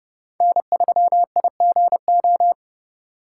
Morse code, heard as a keyed tone, about 38 wpm, 700 Hz. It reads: N3IGO